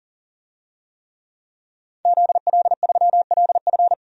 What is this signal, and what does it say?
Morse code, heard as a keyed tone, about 37 wpm, 700 Hz. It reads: ZL3LF
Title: DXCC-100 Call Signs - 5 Characters - 40wpm